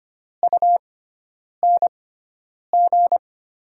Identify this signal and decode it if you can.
Morse code, heard as a keyed tone, about 25 wpm, spaced as for 9 wpm, 700 Hz. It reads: UNG
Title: USA Call Signs - 6 Characters - 6x Character Spacing - 25wpm